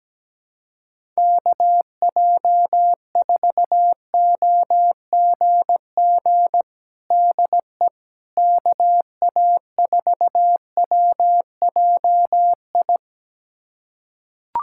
Morse code, heard as a keyed tone, about 17 wpm, 700 Hz. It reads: KJ4OGG DE KA4WJI
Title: USA Callsign DE Callsign - 6 Characters - 17wpm